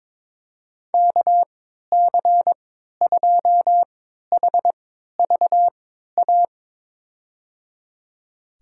Morse code, heard as a keyed tone, about 22 wpm, spaced as for 13 wpm, 700 Hz. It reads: KC2HVA
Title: USA Call Signs - 6 Characters - 3x Character Spacing - 22wpm